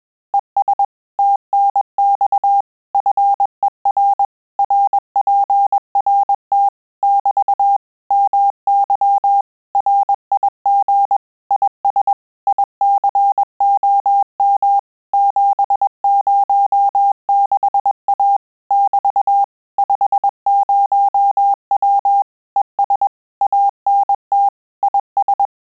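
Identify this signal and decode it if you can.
Morse code, heard as a keyed tone, about 21 wpm, 800 Hz. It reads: ES TNX FER RPRT <BT> MY RIG IS ICOM 706A <BT> 50W ES ANT IS
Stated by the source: Morse QSO practice at 20 words per minute, in Italian and English.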